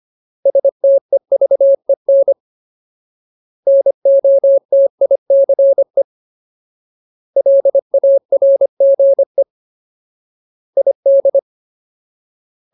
Morse code, heard as a keyed tone, about 25 wpm, 550 Hz. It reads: STEVEN NOTICE LARGE ID
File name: Ultimate Word List / 25WPM Ultimate